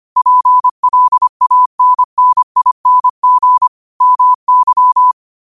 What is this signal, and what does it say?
Morse code, heard as a keyed tone, about 25 wpm, 1000 Hz. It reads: PLANNING MY
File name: qotdfiles / 25wpm